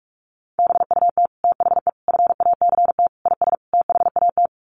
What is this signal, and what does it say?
Morse code, heard as a keyed tone, about 44 wpm, 700 Hz. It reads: BUT THE FACT IS THAT
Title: Sets of 5 Words Encore2 45wpm